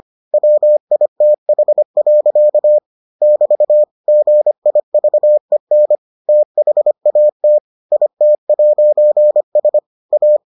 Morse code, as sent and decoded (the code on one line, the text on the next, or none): .-- .. - .... .-.-.- / -...- / --. .. ...- . -. / - .... .- - / .. - .----. ... / .-
WITH. <BT> GIVEN THAT IT'S A